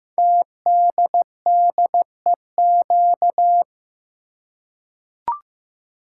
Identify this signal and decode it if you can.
Morse code, heard as a keyed tone, about 15 wpm, 700 Hz. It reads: TDDEQ
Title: Code Groups - 5 Letters - 15wpm